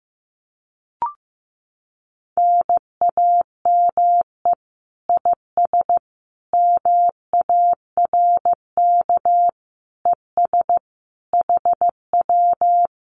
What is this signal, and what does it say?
Morse code, heard as a keyed tone, about 15 wpm, 700 Hz. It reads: NAME IS MARK ES HW
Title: Common QSO Phrases Encore 15wpm